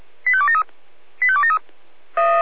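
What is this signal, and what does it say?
Five-tone selective call (ZVEI), heard as quick runs of five tones, tones of about 70 ms. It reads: 85273 85273
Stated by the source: ZVEI selective call received by SDR.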